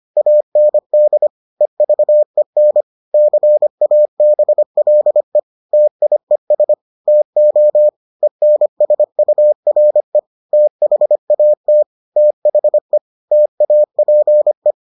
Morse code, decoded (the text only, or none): AND EVEN CABLE TIES TO ENSURE THAT THE TAPE